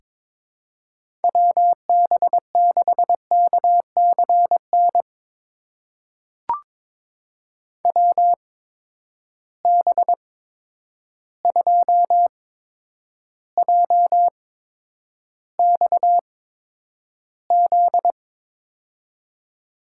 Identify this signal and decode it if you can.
Morse code, heard as a keyed tone, about 22 wpm, spaced as for 9 wpm, 700 Hz. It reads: WB6KCN WB2JXZ
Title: USA Call Signs - 6 Characters - 8x Character Spacing - 22wpm